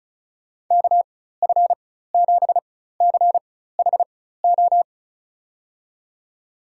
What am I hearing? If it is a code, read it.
Morse code, heard as a keyed tone, about 36 wpm, spaced as for 16 wpm, 700 Hz. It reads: KF7CHO
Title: USA Call Signs - 6 Characters - 4x Character Spacing - 35wpm